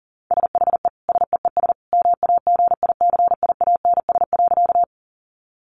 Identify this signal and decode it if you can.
Morse code, heard as a keyed tone, about 40 wpm, 700 Hz. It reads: SHE SEES MAGICIANS.
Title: Easy Phrases 40wpm